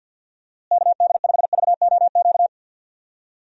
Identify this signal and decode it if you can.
Morse code, heard as a keyed tone, about 47 wpm, 700 Hz. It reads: KD5VOQ
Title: USA Call Signs - 6 Characters - Repeated 2x - 50wpm